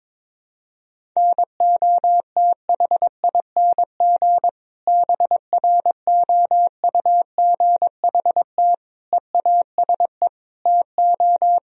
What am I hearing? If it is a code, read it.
Morse code, heard as a keyed tone, about 22 wpm, 700 Hz. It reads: NOTHING BROUGHT EASE TO
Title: Sentences from Top 1000 Words - Review Part 1 - 22wpm